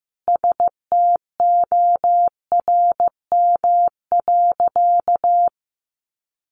Morse code, heard as a keyed tone, about 14 wpm, 700 Hz. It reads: STORM.
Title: Sentences from Top 1500 Words 15wpm